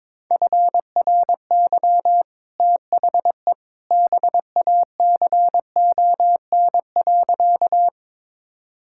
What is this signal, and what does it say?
Morse code, heard as a keyed tone, about 21 wpm, 700 Hz. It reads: FRY THE BACON.
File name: practice / 22wpm / Easy Sentences with Hints - 22wpm